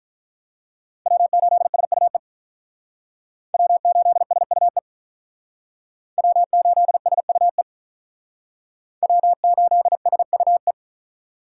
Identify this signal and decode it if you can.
Morse code, heard as a keyed tone, about 47 wpm, 700 Hz. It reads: W8SUE W8SUE W8SUE W8SUE